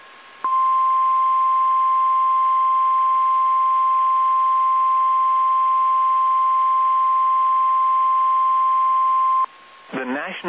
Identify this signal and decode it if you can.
Emergency Alert System (SAME) tones (the 1050 Hz attention tone), with speech. It is heard from 0.4 to 9.5 s.